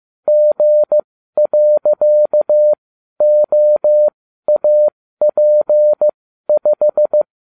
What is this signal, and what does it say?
Morse code, heard as a keyed tone, about 15 wpm, spaced as for 12 wpm, 600 Hz. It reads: G.OAP5